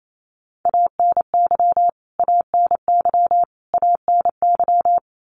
Morse code, heard as a keyed tone, about 28 wpm, 700 Hz. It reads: ANY ANY ANY
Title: Top 200 Words Repeated 28wpm